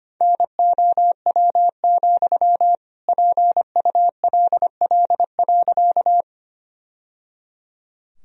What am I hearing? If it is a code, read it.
Morse code, heard as a keyed tone, about 25 wpm, 700 Hz. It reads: NOW, PULL.
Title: Sentences from Top 1000 Words - Review Part 2 - 25wpm